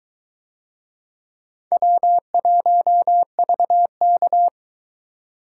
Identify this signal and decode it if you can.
Morse code, heard as a keyed tone, about 21 wpm, 700 Hz. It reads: W1VK